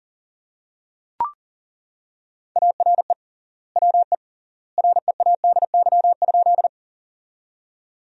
Morse code, heard as a keyed tone, about 37 wpm, 700 Hz. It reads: ARE WE READY?
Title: 3x Word-Spacing - Sentences from Top 400 Words - 40wpm